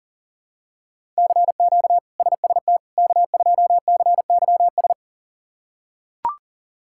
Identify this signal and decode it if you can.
Morse code, heard as a keyed tone, about 40 wpm, 700 Hz. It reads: CQ SST K2CYS
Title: K1USN SST QSO Practice - 40wpm